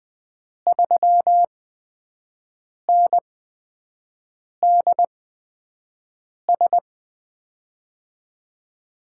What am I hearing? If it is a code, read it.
Morse code, heard as a keyed tone, about 20 wpm, spaced as for 5 wpm, 700 Hz. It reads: 3NDS
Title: USA Call Signs - 6 Characters - 8x Character Spacing - 20wpm